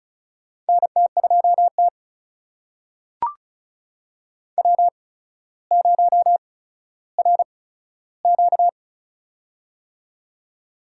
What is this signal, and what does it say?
Morse code, heard as a keyed tone, about 36 wpm, spaced as for 15 wpm, 700 Hz. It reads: NT2T W0RQ
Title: USA Call Signs - 8x Character Spacing - 35wpm